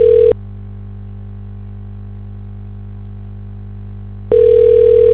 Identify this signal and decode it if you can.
Ringback tone (North America), 440 + 480 Hz.